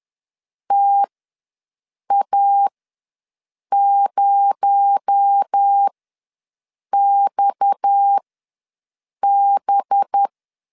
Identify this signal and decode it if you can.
Morse code, heard as a keyed tone, about 10 wpm, spaced as for 6 wpm, 800 Hz. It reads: TA0XB
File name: LEZIONI / 10 SIGNS / 10D COMA ws 10wpm 4314